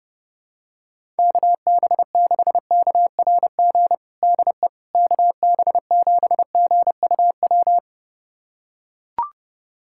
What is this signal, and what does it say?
Morse code, heard as a keyed tone, about 32 wpm, 700 Hz. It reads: KB6KRG DE KB7GUW
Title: USA Callsign DE Callsign - 6 Characters - 30wpm